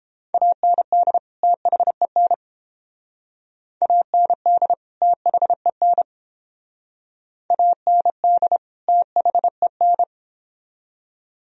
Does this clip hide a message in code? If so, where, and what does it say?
Yes: Morse code from 0.3 to 10.0 s, heard as a keyed tone, about 28 wpm, 700 Hz. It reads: AND THEN AND THEN AND THEN